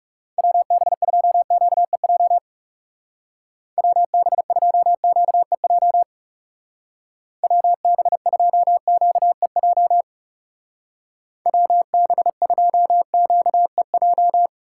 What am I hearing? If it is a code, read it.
Morse code, heard as a keyed tone, about 35 wpm, 700 Hz. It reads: WB2QEJ WB2QEJ WB2QEJ WB2QEJ
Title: Speed Racing - USA 2x3 Call Signs 30wpm